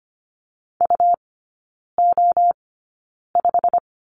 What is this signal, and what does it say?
Morse code, heard as a keyed tone, about 25 wpm, spaced as for 9 wpm, 700 Hz. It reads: UO5
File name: Lesson 17e - Letter u - Three Characters 25wpm